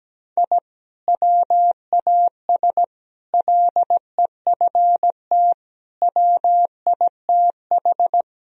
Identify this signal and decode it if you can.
Morse code, heard as a keyed tone, about 17 wpm, 700 Hz. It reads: I WAS LEFT WITH
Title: Sentences from Top 1000 Words - Review Part 1 - 17wpm